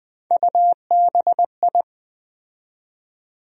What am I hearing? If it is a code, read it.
Morse code, heard as a keyed tone, about 20 wpm, 700 Hz. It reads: UBI
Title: USA Call Sign Postfixes - 3 Letters 20wpm